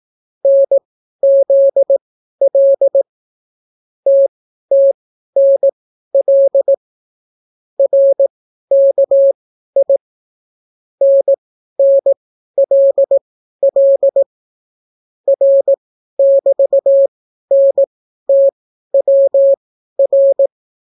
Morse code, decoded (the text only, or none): NZL TTNL RKI NNLL R<BT>NTWR